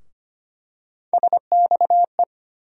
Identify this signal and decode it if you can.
Morse code, heard as a keyed tone, about 25 wpm, 700 Hz. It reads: SXE
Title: USA Call Sign Postfixes - 3 Letters 25wpm